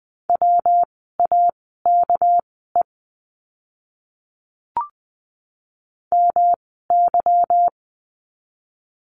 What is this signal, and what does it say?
Morse code, heard as a keyed tone, about 20 wpm, spaced as for 14 wpm, 700 Hz. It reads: WAKE MY
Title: 199 Lesson 32f - Letter k - Words 20wpm